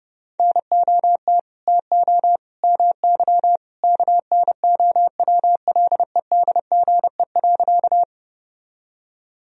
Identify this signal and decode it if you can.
Morse code, heard as a keyed tone, about 30 wpm, 700 Hz. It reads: NOT TO MY KNOWLEDGE.
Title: Sentences from Top 1000 Words - Review Part 3 - 30wpm